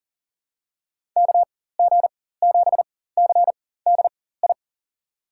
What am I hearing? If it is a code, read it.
Morse code, heard as a keyed tone, about 38 wpm, spaced as for 18 wpm, 700 Hz. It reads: KG7CDI
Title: USA Call Signs - 6 Characters - 4x Character Spacing - 40wpm